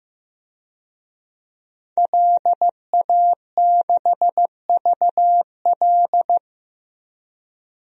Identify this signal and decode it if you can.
Morse code, heard as a keyed tone, about 15 wpm, 700 Hz. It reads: LA6VL
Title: DXCC-100 Call Signs - 5 Characters - Repeated 3x - 15wpm